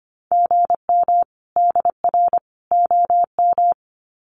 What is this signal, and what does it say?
Morse code, heard as a keyed tone, about 25 wpm, 700 Hz. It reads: GM DR OM
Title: Common QSO Phrases 25wpm